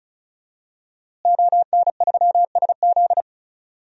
Morse code, heard as a keyed tone, about 36 wpm, 700 Hz. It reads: ON3SZ
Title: DXCC-100 Call Signs - 5 Characters - Repeated 3x - 35wpm